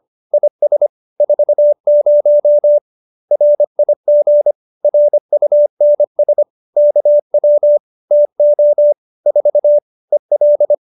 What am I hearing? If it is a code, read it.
Morse code, heard as a keyed tone, about 25 wpm, 590 Hz. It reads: IS 40 RIG RUNS KW TO 4 EL
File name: qso201b 25